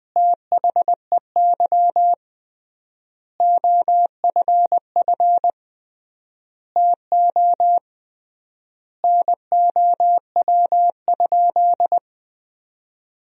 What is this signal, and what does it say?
Morse code, heard as a keyed tone, about 21 wpm, 700 Hz. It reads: THEY OFF TO NOW?
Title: 3x Word-Spacing - Sentences from Top 200 Words - 20wpm